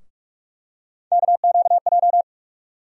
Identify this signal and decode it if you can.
Morse code, heard as a keyed tone, about 42 wpm, 700 Hz. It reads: KQJ
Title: USA Call Sign Postfixes - 3 Letters 45wpm